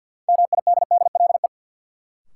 Morse code, heard as a keyed tone, about 47 wpm, 700 Hz. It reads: MIDDLE
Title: Sets of 2 Words - New Word Drill - 50wpm